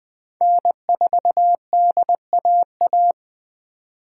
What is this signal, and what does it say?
Morse code, heard as a keyed tone, about 19 wpm, 700 Hz. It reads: N4DAA